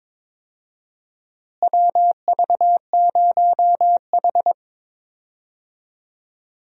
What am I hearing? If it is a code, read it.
Morse code, heard as a keyed tone, about 22 wpm, 700 Hz. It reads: WV0H
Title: USA Call Signs - 4 Characters - Repeated 3x - 22wpm